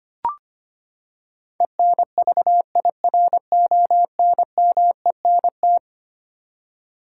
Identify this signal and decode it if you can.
Morse code, heard as a keyed tone, about 25 wpm, 700 Hz. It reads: ENVIRONMENT
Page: morse-code-ninja-practice-25wpm